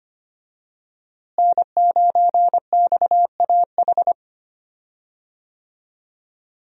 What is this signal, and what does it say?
Morse code, heard as a keyed tone, about 25 wpm, 700 Hz. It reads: N9XAH
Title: USA Call Signs - 5 Characters - Repeated 2x - 25wpm